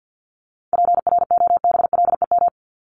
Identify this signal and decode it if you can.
Morse code, heard as a keyed tone, about 48 wpm, 700 Hz. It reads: PROBLEM
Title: ES:S Suffix - Top 100 Words with Base Words 50wpm